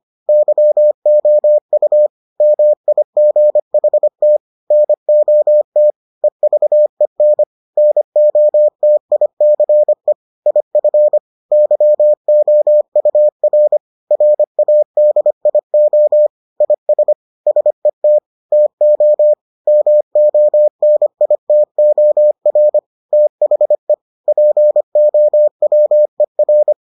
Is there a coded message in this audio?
Yes: Morse code from 0.3 to 26.7 s, heard as a keyed tone, about 25 wpm, 600 Hz. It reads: YOU MIGHT NOT EVEN NOTICE IF YOUR RADIO IS SET TO MONITOR THE POWER